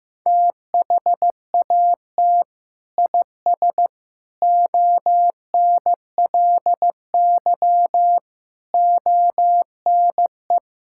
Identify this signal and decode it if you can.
Morse code, heard as a keyed tone, about 15 wpm, 700 Hz. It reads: THAT IS ONLY ONE
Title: Sentences from Top 1000 Words - Review Part 5 - 15wpm